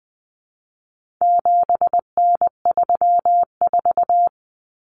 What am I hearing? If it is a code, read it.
Morse code, heard as a keyed tone, about 20 wpm, 700 Hz. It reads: 7N34
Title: Common Cut Numbers - 4-Digit 20wpm